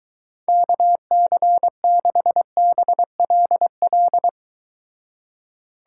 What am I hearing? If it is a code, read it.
Morse code, heard as a keyed tone, about 23 wpm, 700 Hz. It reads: KC6BLL